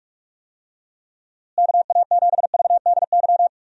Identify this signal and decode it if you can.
Morse code, heard as a keyed tone, about 46 wpm, 700 Hz. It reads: KA7VDY
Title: USA Call Signs - 6 Characters - Repeated 2x - 45wpm